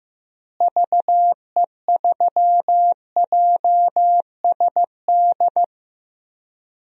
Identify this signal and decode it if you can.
Morse code, heard as a keyed tone, about 15 wpm, 700 Hz. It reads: VE3JSD
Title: DXCC-100 Call Signs - 6 Characters - Repeated 3x - 15wpm